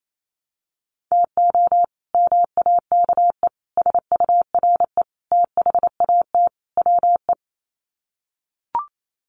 Morse code, heard as a keyed tone, about 28 wpm, 700 Hz. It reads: TO MAKE SURE THAT WE
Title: Sets of 5 Words Encore 28wpm